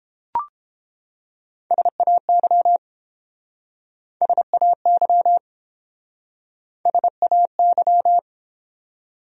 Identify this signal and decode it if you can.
Morse code, heard as a keyed tone, about 29 wpm, 700 Hz. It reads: SAY SAY SAY